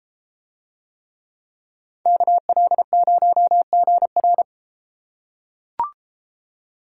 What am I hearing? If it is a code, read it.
Morse code, heard as a keyed tone, about 33 wpm, 700 Hz. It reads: KL0GR